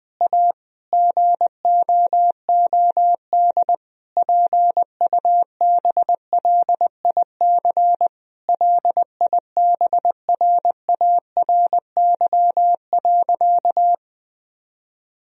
Morse code, heard as a keyed tone, about 20 wpm, 700 Hz. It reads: A GOOD PUBLIC LIBRARY.
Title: Sentences from Top 1000 Words - Review Part 4 - 20wpm